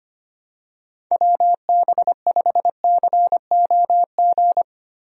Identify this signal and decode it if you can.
Morse code, heard as a keyed tone, about 25 wpm, 700 Hz. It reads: WB5COG
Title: USA Call Signs - 6 Characters - Repeated 3x - 25wpm